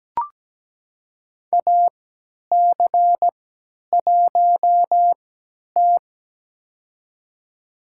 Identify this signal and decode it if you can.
Morse code, heard as a keyed tone, about 17 wpm, spaced as for 10 wpm, 700 Hz. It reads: AC1T
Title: USA Call Signs - 4 Characters - 3x Character Spacing - 17wpm